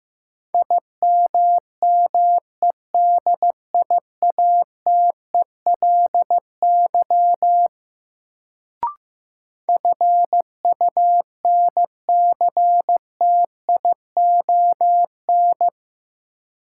Morse code, heard as a keyed tone, about 15 wpm, 700 Hz. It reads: IMMEDIATELY FUNCTION
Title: Sentences from Top 1100 Words - New Word Drill Alt - 15wpm